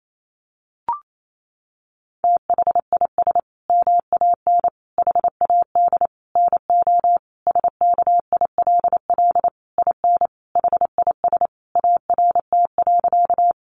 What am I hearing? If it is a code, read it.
Morse code, heard as a keyed tone, about 28 wpm, 700 Hz. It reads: THIS MAN HAD NO SKILL IN HIS ART.
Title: Sentences from Top 1200 Words 28wpm